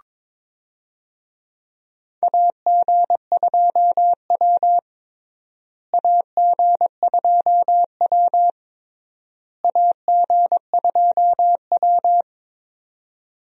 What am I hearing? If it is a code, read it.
Morse code, heard as a keyed tone, about 22 wpm, 700 Hz. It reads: AG2W AG2W AG2W
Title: USA Call Signs - 4 Characters - Repeated 3x - 22wpm